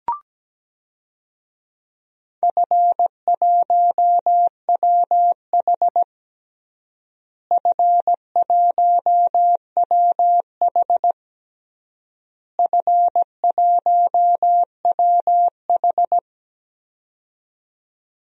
Morse code, heard as a keyed tone, about 17 wpm, 700 Hz. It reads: F1WH F1WH F1WH
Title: DXCC-100 Call Signs - 4 Characters - Repeated 3x - 17wpm